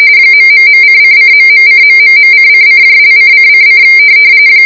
RTTY, about 45.45 baud. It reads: ...:::..:...:::)::::